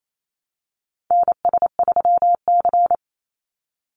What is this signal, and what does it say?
Morse code, heard as a keyed tone, about 28 wpm, 700 Hz. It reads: NS3C
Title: Call Signs Repeated 2x - 28wpm